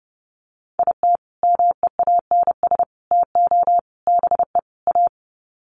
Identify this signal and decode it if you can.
Morse code, heard as a keyed tone, about 30 wpm, 700 Hz. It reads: IT MEANS TO BE A
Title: Sets of 5 Words Encore2 30wpm